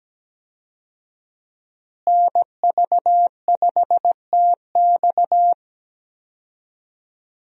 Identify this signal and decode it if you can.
Morse code, heard as a keyed tone, about 17 wpm, 700 Hz. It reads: NV5TX
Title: USA Call Signs - 5 Characters - Repeated 2x - 17wpm